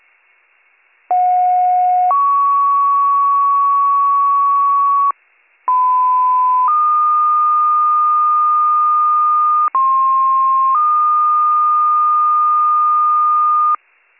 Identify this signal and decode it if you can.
Two-tone page, heard as a one-second tone, then a three-second tone, about 730 then 1120 Hz, about 1010 then 1220 Hz, about 1060 then 1220 Hz. It is heard from 1.1 to 13.8 s.